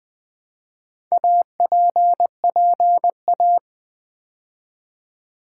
Morse code, heard as a keyed tone, about 20 wpm, 700 Hz. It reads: APPA
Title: Build A Word - 2 Letters At A Time - 8-Letter Words - 20wpm